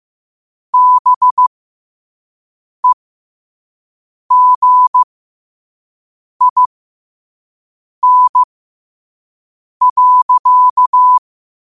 Morse code, heard as a keyed tone, about 15 wpm, spaced as for 5 wpm, 1000 Hz. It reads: BEGIN.
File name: qotdfiles / 5wpm